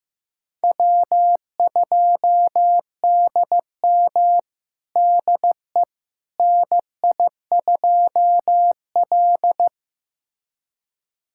Morse code, heard as a keyed tone, about 16 wpm, 700 Hz. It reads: W2DM DE NI2L
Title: USA Callsign DE Callsign - 4 Characters - 15wpm